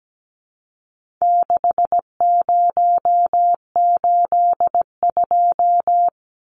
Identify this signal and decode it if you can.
Morse code, heard as a keyed tone, about 17 wpm, 700 Hz. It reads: 6082
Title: Numbers - 4-Digit 17wpm